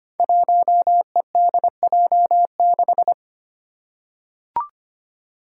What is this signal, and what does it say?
Morse code, heard as a keyed tone, about 25 wpm, 700 Hz. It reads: 1EDJ6
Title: Five Challenging Letter-Number Combinations 25wpm